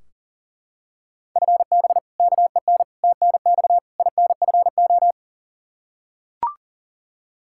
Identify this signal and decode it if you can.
Morse code, heard as a keyed tone, about 37 wpm, 700 Hz. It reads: FB KEN TNX INFO
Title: Practice QSO - Exchange 3 - More Info - 40wpm